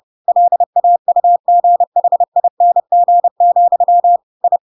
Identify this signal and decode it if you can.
Morse code, heard as a keyed tone, about 30 wpm, 700 Hz. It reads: LAUGHING, I
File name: Pinocchio / Pinocchio ch3 30 30 ik1hge